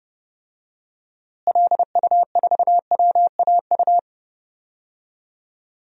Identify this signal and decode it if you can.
Morse code, heard as a keyed tone, about 30 wpm, 700 Hz. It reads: LU4WAU
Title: DXCC-100 Call Signs - 6 Characters - Repeated 3x - 30wpm